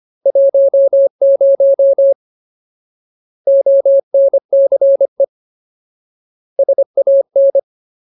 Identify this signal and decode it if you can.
Morse code, heard as a keyed tone, about 25 wpm, 550 Hz. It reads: 10 ONCE SAN
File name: Ultimate Word List / 25WPM Ultimate